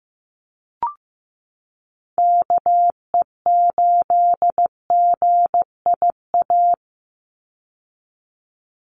Morse code, heard as a keyed tone, about 15 wpm, 700 Hz. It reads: KE8GIA
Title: USA Call Signs 15wpm